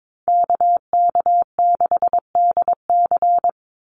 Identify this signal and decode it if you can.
Morse code, heard as a keyed tone, about 22 wpm, 700 Hz. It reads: KK6DC
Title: USA Call Signs 22wpm